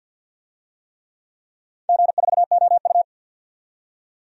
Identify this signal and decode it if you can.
Morse code, heard as a keyed tone, about 53 wpm, 700 Hz. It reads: G4OU
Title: DXCC-100 Call Signs - 4 Characters - 50wpm